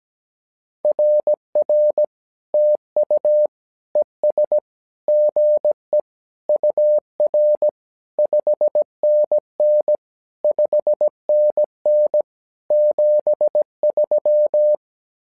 Morse code, heard as a keyed tone, about 17 wpm, 600 Hz. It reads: RR TU ES GE UR 5NN 5NN 73
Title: SOTA QSO Practice 17wpm